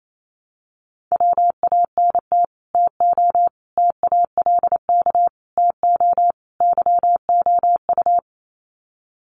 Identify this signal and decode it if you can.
Morse code, heard as a keyed tone, about 28 wpm, 700 Hz. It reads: WANT TO TALK TO YOU
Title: Sets of 5 Words Encore 28wpm